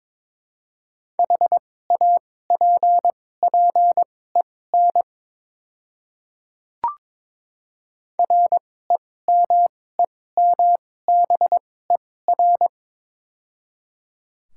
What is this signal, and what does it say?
Morse code, heard as a keyed tone, about 22 wpm, spaced as for 16 wpm, 700 Hz. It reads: HAPPEN REMEMBER
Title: 2x Character Spacing - Top 500 Words - 22wpm